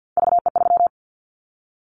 Morse code, heard as a keyed tone, about 50 wpm, 700 Hz. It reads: VE3
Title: Callsign Prefixes for Top 25 DXCC - 50wpm